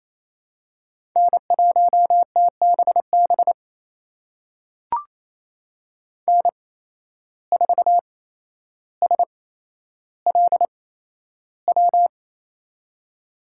Morse code, heard as a keyed tone, about 28 wpm, spaced as for 12 wpm, 700 Hz. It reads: N1TBB N4SLW
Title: USA Call Signs - 5 Characters - 8x Character Spacing - 28wpm